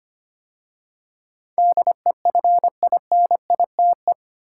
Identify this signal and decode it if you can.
Morse code, heard as a keyed tone, about 25 wpm, 700 Hz. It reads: DEFINITE
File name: 8 Letter Words - First 3 Letters Given - 25wpm